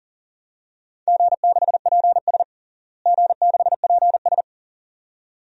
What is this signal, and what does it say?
Morse code, heard as a keyed tone, about 40 wpm, 700 Hz. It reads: G6PS G6PS
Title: DXCC-100 Call Signs - 4 Characters - Repeated 2x - 40wpm